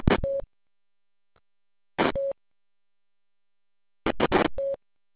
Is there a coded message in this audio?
No.